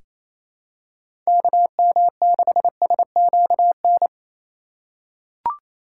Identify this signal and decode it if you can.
Morse code, heard as a keyed tone, about 28 wpm, 700 Hz. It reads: KM6SQN
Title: USA Call Signs - 6 Characters - 2x Character Spacing - 28wpm